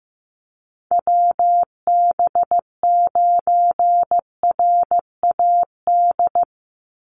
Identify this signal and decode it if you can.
Morse code, heard as a keyed tone, about 15 wpm, 700 Hz. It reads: WB9RAD
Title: USA Call Signs 15wpm